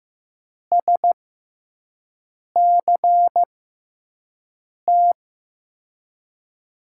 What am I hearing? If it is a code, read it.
Morse code, heard as a keyed tone, about 15 wpm, spaced as for 5 wpm, 700 Hz. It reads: SCT
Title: USA Call Signs - 5 Characters - 6x Character Spacing - 15wpm